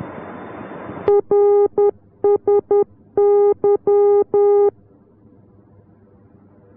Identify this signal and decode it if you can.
Morse code, heard as a keyed tone, about 10 wpm, 410 Hz. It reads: RSY